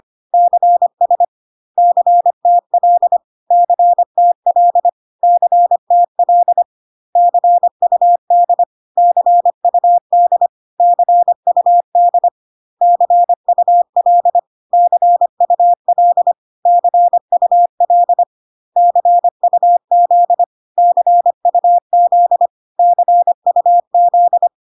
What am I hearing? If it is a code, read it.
Morse code, heard as a keyed tone, about 25 wpm, 700 Hz. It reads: CS CTL CTL CTL CUD CUD CUD CUL CUL CUL CUZ CUZ CUZ
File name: CW abbreviations alphabetical repeated 3 times 25 25wpm